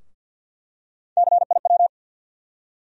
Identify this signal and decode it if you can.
Morse code, heard as a keyed tone, about 50 wpm, 700 Hz. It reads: CIW
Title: USA Call Sign Postfixes - 3 Letters 50wpm